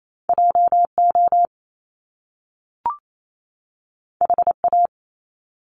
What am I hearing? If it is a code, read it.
Morse code, heard as a keyed tone, about 28 wpm, 700 Hz. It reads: JO HA
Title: Top Two Letter Combinations in Words 28wpm